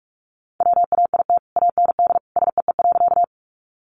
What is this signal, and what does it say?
Morse code, heard as a keyed tone, about 44 wpm, 700 Hz. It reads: WAIT AND SEE.
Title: Sentences from Top 500 Words 45wpm